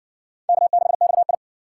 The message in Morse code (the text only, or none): K6XI